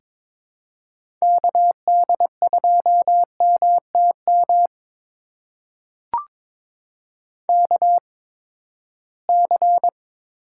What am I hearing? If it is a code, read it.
Morse code, heard as a keyed tone, about 22 wpm, spaced as for 15 wpm, 700 Hz. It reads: KD2MTM KC
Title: USA Call Signs - 6 Characters - 8x Character Spacing - 22wpm